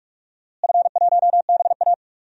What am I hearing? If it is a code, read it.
Morse code, heard as a keyed tone, about 46 wpm, 700 Hz. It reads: F1BA
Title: DXCC-100 Call Signs - 4 Characters - Repeated 2x - 45wpm